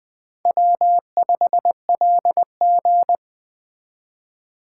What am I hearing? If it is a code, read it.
Morse code, heard as a keyed tone, about 21 wpm, 700 Hz. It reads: W5LG
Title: Speed Racing - USA Call Signs 17wpm